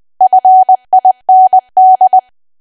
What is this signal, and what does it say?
Morse code, heard as a keyed tone, about 20 wpm, 750 Hz. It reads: FIND
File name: cwa / W1AW-Level3 / 20WPM